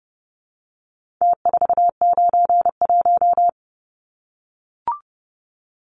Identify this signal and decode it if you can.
Morse code, heard as a keyed tone, about 30 wpm, 700 Hz. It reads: T491
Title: Numbers - Contest Style - 4-Digit 30wpm